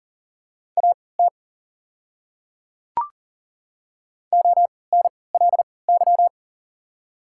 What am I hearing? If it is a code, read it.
Morse code, heard as a keyed tone, about 40 wpm, spaced as for 23 wpm, 690 Hz. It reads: AT ONLY
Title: 3x Character Spacing - Top 200 Words - 40wpm